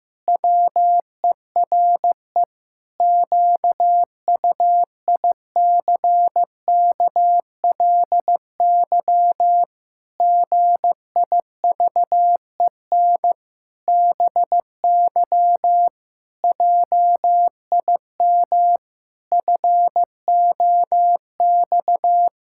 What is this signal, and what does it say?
Morse code, heard as a keyed tone, about 15 wpm, 700 Hz. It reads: WERE QUICKLY GIVEN BY JIM FOX
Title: Pangram Challenge v2 - 15wpm